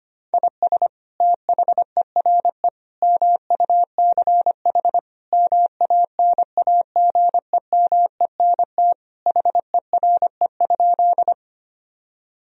Morse code, heard as a keyed tone, about 25 wpm, 700 Hz. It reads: IS THERE MUCH MANAGEMENT HERE?
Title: Sentences from Top 1000 Words - Review Part 4 - 25wpm